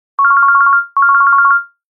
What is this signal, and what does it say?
There is no code.